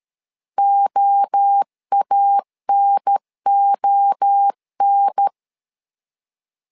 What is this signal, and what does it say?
Morse code, heard as a keyed tone, about 13 wpm, 800 Hz. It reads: OANON
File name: LEZIONI / 3 MWRO / 3 MWRO 12wpm 1730